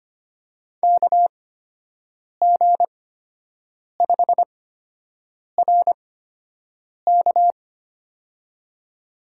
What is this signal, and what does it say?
Morse code, heard as a keyed tone, about 25 wpm, spaced as for 7 wpm, 700 Hz. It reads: KG5RK
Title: USA Call Signs - 8x Character Spacing - 25wpm